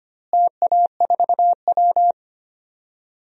Morse code, heard as a keyed tone, about 25 wpm, 700 Hz. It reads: TA4W
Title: DXCC-100 Call Signs - 4 Characters - Repeated 3x - 25wpm